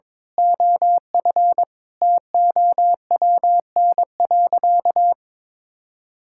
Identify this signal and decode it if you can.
Morse code, heard as a keyed tone, about 22 wpm, 700 Hz. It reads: OF TOWN.
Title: Sentences from Top 1000 Words - Review Part 6 - 22wpm